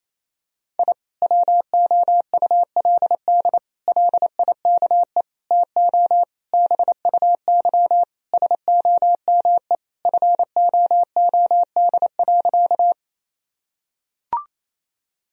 Morse code, heard as a keyed tone, about 28 wpm, 700 Hz. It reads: I WOULD LIKE TO BUY SOME FOOD.
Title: Sentences from Top 1000 Words - Review Part 4 - 28wpm